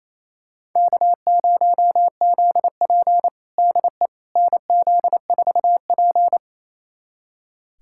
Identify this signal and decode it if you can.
Morse code, heard as a keyed tone, about 28 wpm, 700 Hz. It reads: K0ZP DE NZ4P
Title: USA Callsign DE Callsign - 4 Characters - 28wpm